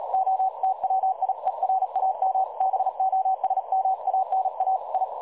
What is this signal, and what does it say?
Morse code, heard as a keyed tone, about 38 wpm, 760 Hz. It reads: OTWIERAM BOSMANA I
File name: dlazaawansowanych / nagraniazpasma / websdr recording 3565.0kHz